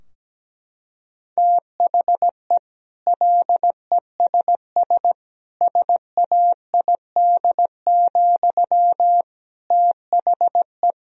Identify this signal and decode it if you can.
Morse code, heard as a keyed tone, about 17 wpm, 700 Hz. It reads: THE LESS SAID, THE
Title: Sentences from Top 1000 Words - Review Part 2 - 17wpm